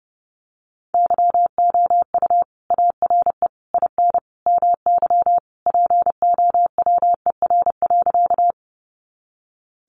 Morse code, heard as a keyed tone, about 30 wpm, 700 Hz. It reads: YOU ARE IN MY POWER.
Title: Sentences from Top 500 Words 30wpm